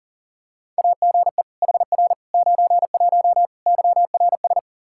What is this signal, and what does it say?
Morse code, heard as a keyed tone, about 37 wpm, 700 Hz. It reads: AGE HR 91 YRS
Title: Practice QSO - Exchange 3 - More Info - 40wpm